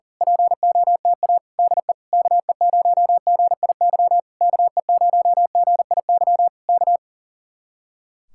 Morse code, heard as a keyed tone, about 38 wpm, 700 Hz. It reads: POTA DE KE0GIY KE0GIY K
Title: Speed-Racing Common QSO Phrases Encore - 40wpm